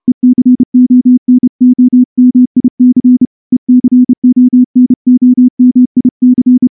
Morse code, heard as a keyed tone, about 29 wpm, 250 Hz. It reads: ECONOMIC ECONOMIC
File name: morse-training-30WPM